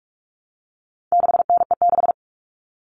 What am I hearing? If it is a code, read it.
Morse code, heard as a keyed tone, about 44 wpm, 700 Hz. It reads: 6NE6
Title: All Cut Numbers - 4-Digit 45wpm